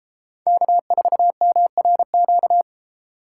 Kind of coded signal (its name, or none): Morse code